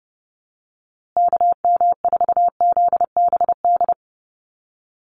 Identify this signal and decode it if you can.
Morse code, heard as a keyed tone, about 30 wpm, 700 Hz. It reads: KM4ZBD
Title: USA Call Signs Repeated 3x 30wpm